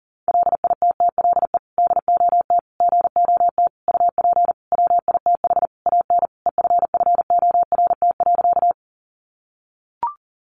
Morse code, heard as a keyed tone, about 40 wpm, 700 Hz. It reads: LITTLE DOT GOT UP WITH AN EFFORT.
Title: Sentences from Top 1500 Words 40wpm